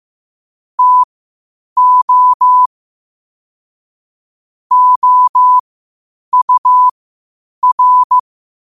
Morse code, heard as a keyed tone, about 15 wpm, spaced as for 8 wpm, 1000 Hz. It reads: TO OUR
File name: qotdfiles / 8wpm